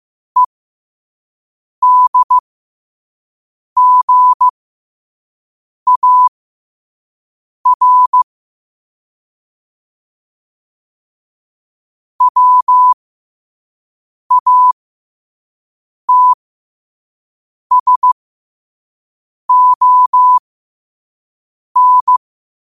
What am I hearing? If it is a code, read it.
Morse code, heard as a keyed tone, about 15 wpm, spaced as for 5 wpm, 1000 Hz. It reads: EDGAR WATSON
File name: qotdfiles / 5wpm